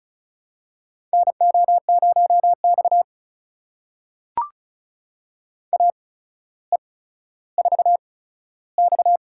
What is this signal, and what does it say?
Morse code, heard as a keyed tone, about 35 wpm, spaced as for 15 wpm, 700 Hz. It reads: NO0X AE4X
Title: USA Call Signs - 4 Characters - 8x Character Spacing - 35wpm